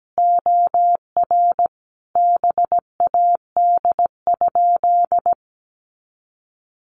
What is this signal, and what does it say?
Morse code, heard as a keyed tone, about 17 wpm, 700 Hz. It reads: OR BAD?
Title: Sentences from Top 700 Words 17wpm